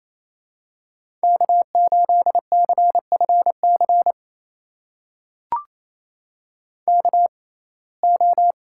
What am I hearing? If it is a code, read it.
Morse code, heard as a keyed tone, about 28 wpm, spaced as for 20 wpm, 700 Hz. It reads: K8CFC KO
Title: USA Call Signs - 5 Characters - 6x Character Spacing - 28wpm